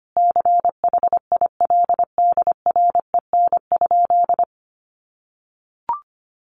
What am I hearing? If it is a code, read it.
Morse code, heard as a keyed tone, about 25 wpm, 700 Hz. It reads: CHILDREN?
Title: Sentences from Top 1500 Words 25wpm